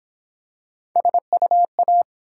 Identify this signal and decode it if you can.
Morse code, heard as a keyed tone, about 31 wpm, 700 Hz. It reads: SUA